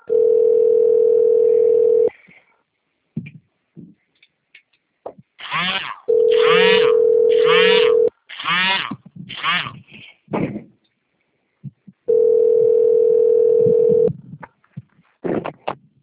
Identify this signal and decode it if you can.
Ringback tone (North America), 440 + 480 Hz.